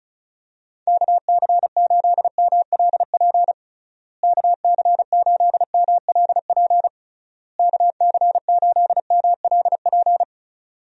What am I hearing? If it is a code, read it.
Morse code, heard as a keyed tone, about 36 wpm, 700 Hz. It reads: KC8MLP KC8MLP KC8MLP
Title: USA Call Signs - 6 Characters - Repeated 3x - 35wpm